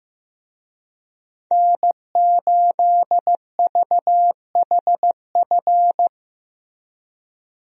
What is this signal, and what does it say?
Morse code, heard as a keyed tone, about 15 wpm, 700 Hz. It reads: N8VHF
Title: USA Call Signs - 5 Characters - Repeated 2x - 15wpm